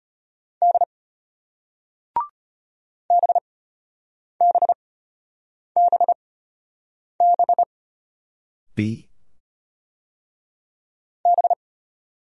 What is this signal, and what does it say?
Morse code, heard as a keyed tone, about 33 wpm, spaced as for 8 wpm, 700 Hz. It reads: D BBBB B